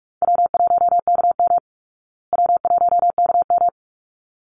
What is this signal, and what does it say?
Morse code, heard as a keyed tone, about 44 wpm, 700 Hz. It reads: W1KM W1KM
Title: Call Signs Repeated 3x - 45wpm